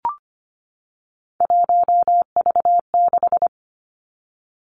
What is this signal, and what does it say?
Morse code, heard as a keyed tone, about 25 wpm, 700 Hz. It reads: 1V6
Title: Three Challenging Letter-Number Combinations 25wpm